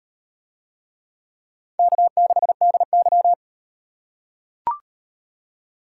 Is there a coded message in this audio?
Yes: Morse code from 1.8 to 3.3 s, heard as a keyed tone, about 38 wpm, 700 Hz. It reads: K6DY